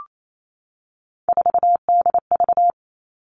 Morse code, heard as a keyed tone, about 28 wpm, 700 Hz. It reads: 4DV
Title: Three Challenging Letter-Number Combinations 28wpm